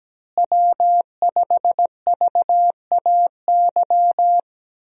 Morse code, heard as a keyed tone, about 17 wpm, 700 Hz. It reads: W5VAY
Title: USA Call Signs - 5 Characters - Repeated 2x - 17wpm